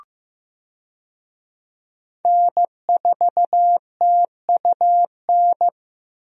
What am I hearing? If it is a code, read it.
Morse code, heard as a keyed tone, about 15 wpm, 700 Hz. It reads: N4TUN
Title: USA Call Signs - 5 Characters - 15wpm